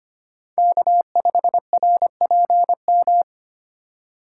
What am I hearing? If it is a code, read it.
Morse code, heard as a keyed tone, about 25 wpm, 700 Hz. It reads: K5RPM
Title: USA Call Signs - 5 Characters - 4x Character Spacing - 25wpm